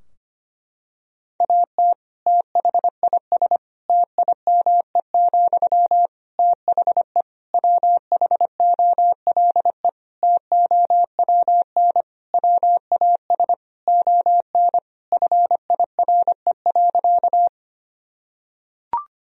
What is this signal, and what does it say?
Morse code, heard as a keyed tone, about 25 wpm, 700 Hz. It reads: AT THIS TIME, THE WHOLE TOWN WAS ON FIRE.
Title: Sentences from Top 1000 Words - Review Part 2 - 25wpm